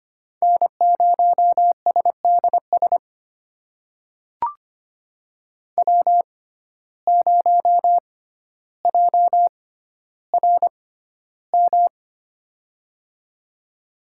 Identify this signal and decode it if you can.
Morse code, heard as a keyed tone, about 25 wpm, spaced as for 13 wpm, 700 Hz. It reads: N0SDS W0JRM
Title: USA Call Signs - 5 Characters - 6x Character Spacing - 25wpm